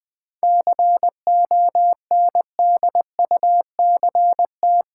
Morse code, heard as a keyed tone, about 20 wpm, 700 Hz. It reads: CONDUCT
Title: 7 Letter Words - First 3 Letters Given - 20wpm